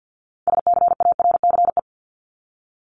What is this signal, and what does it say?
Morse code, heard as a keyed tone, about 50 wpm, 700 Hz. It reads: SCARCE
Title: LY Suffix - Top 100 Words with Base Words 50wpm